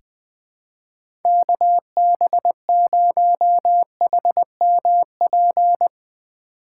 Morse code, heard as a keyed tone, about 20 wpm, 700 Hz. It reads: KB0HMP
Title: USA Call Signs - 6 Characters - 3x Character Spacing - 20wpm